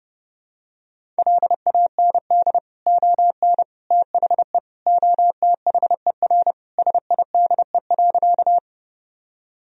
Morse code, heard as a keyed tone, about 32 wpm, 700 Hz. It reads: LAND ON THE OTHER SIDE.
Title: Sentences from Top 1000 Words - Review Part 1 - 30wpm